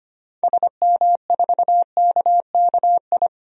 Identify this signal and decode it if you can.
Morse code, heard as a keyed tone, about 25 wpm, 700 Hz. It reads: SM4KKI
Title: DXCC-100 Call Signs - 6 Characters - 25wpm